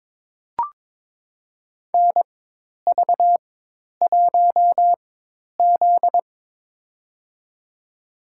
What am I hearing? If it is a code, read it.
Morse code, heard as a keyed tone, about 21 wpm, spaced as for 10 wpm, 700 Hz. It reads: NV1Z
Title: USA Call Signs - 4 Characters - 4x Character Spacing - 22wpm